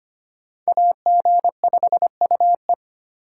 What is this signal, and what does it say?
Morse code, heard as a keyed tone, about 33 wpm, 700 Hz. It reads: AG5UE